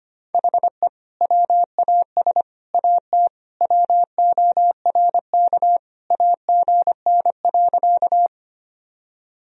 Morse code, heard as a keyed tone, about 25 wpm, 700 Hz. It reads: HE WAS AT WORK AGN.
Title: Sentences from Top 200 Words and Abbreviations 25wpm